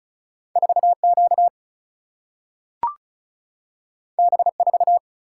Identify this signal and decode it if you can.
Morse code, heard as a keyed tone, about 36 wpm, 700 Hz. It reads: 4 Q B 4
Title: Single-Letter Single-Number 35wpm